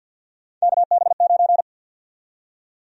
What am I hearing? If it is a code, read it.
Morse code, heard as a keyed tone, about 52 wpm, 700 Hz. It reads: KB9
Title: USA Call Sign Prefixes - 50wpm